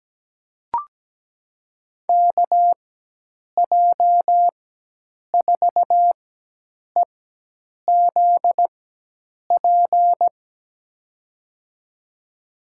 Morse code, heard as a keyed tone, about 17 wpm, spaced as for 8 wpm, 700 Hz. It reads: KJ4EZP
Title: USA Call Signs - 6 Characters - 4x Character Spacing - 17wpm